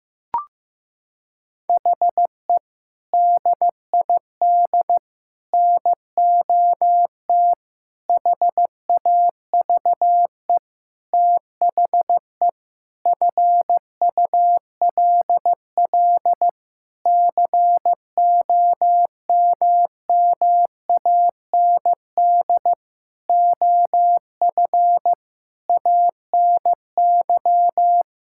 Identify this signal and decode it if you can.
Morse code, heard as a keyed tone, about 15 wpm, 700 Hz. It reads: HE DID NOT HAVE THE FULL COMMAND OF ANY
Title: Sentences from Top 1600 Words 15wpm